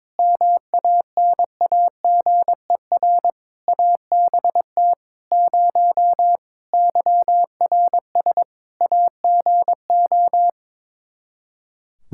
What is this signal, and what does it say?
Morse code, heard as a keyed tone, about 21 wpm, 700 Hz. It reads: MANAGER ABT 0 YRS AGO
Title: Practice QSO - Exchange 3 - More Info - 22wpm